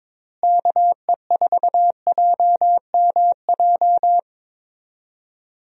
Morse code, heard as a keyed tone, about 22 wpm, 700 Hz. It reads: KE4JMJ
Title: USA Call Signs - 6 Characters - Repeated 2x - 22wpm